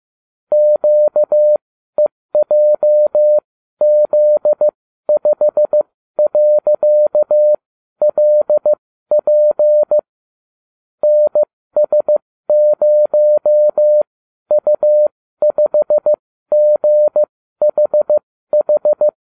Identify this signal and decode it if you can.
Morse code, heard as a keyed tone, about 15 wpm, spaced as for 12 wpm, 600 Hz. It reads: QEJZ5.LP NS0U5GHH
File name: cw / ubacw29 2